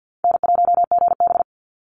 Morse code, heard as a keyed tone, about 48 wpm, 700 Hz. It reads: N1GB
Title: NA SOTA Chaser Callsigns 50wpm